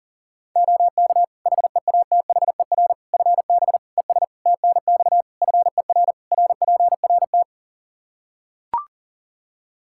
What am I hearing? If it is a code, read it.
Morse code, heard as a keyed tone, about 37 wpm, 700 Hz. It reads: OK HEATHER FB ES TNX FER RPRT
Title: K1USN SST QSO Practice 40wpm